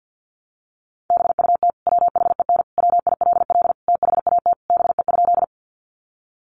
Morse code, heard as a keyed tone, about 50 wpm, 700 Hz. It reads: BUT WHEN WILL THAT BE?
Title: Sentences from Top 100 Words 50wpm